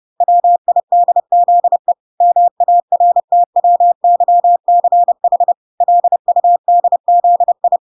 Morse code, heard as a keyed tone, about 30 wpm, 700 Hz. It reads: WIDZE MARTWYCH LUDZI